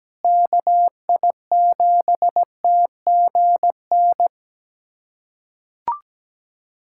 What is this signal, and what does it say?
Morse code, heard as a keyed tone, about 17 wpm, 700 Hz. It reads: KI7TGN
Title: USA Callsign DE Callsign - 6 Characters - 17wpm